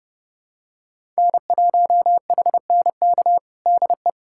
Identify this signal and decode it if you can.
Morse code, heard as a keyed tone, about 30 wpm, 700 Hz. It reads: N1HNK DE
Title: USA Callsign DE Callsign - 5 Characters - 30wpm